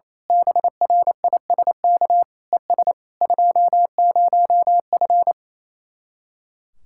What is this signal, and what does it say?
Morse code, heard as a keyed tone, about 26 wpm, 700 Hz. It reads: BRISK ES 20F
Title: Speed-Racing Common QSO Phrases - 28wpm